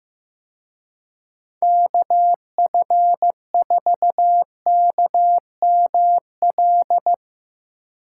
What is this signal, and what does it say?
Morse code, heard as a keyed tone, about 16 wpm, 700 Hz. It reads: KF4KML
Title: Speed Racing - USA 2x3 Call Signs 15wpm